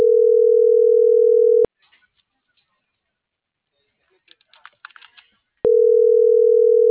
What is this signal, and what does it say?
Ringback tone (North America), 440 + 480 Hz.
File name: CENTRAL MABE / out miriam.soledad - .74674